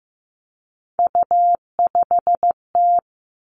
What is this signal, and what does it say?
Morse code, heard as a keyed tone, about 15 wpm, 700 Hz. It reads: U5T
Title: Three Challenging Letter-Number Combinations 15wpm